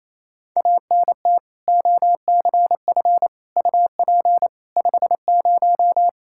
Morse code, heard as a keyed tone, about 27 wpm, 700 Hz. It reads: ANT OCF UP 50
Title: K1USN SST QSO Practice 28wpm